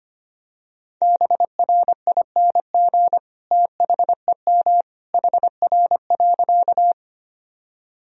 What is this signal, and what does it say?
Morse code, heard as a keyed tone, about 25 wpm, 700 Hz. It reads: BRING THEM HR.
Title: Sentences from Top 500 Words and Abbreviations 25wpm